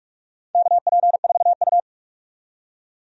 Morse code, heard as a keyed tone, about 44 wpm, 700 Hz. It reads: KP4U
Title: DXCC-100 Call Signs - 4 Characters - 45wpm